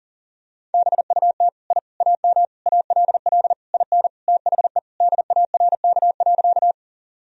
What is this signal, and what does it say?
Morse code, heard as a keyed tone, about 37 wpm, 700 Hz. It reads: BUT I AM ALL IN THE DARK.
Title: Sentences from Top 1000 Words - Review Part 2 - 40wpm